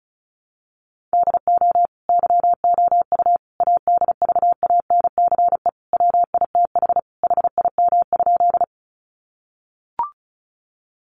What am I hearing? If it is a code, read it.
Morse code, heard as a keyed tone, about 34 wpm, 700 Hz. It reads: DO YOU ADVANCE WITH HIM?
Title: Sentences from Top 1400 Words 35wpm